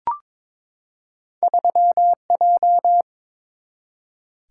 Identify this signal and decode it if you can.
Morse code, heard as a keyed tone, about 22 wpm, 700 Hz. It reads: 3J
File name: practice / 22wpm / Two Letter-Number Combinations - 22wpm